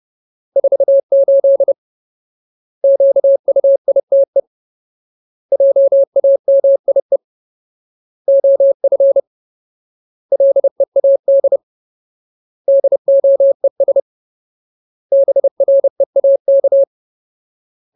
Morse code, heard as a keyed tone, about 30 wpm, 550 Hz. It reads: 48 QUITE JAMIE OF LEAD DOES BREAK
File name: Ultimate Word List / 30WPM Ultimate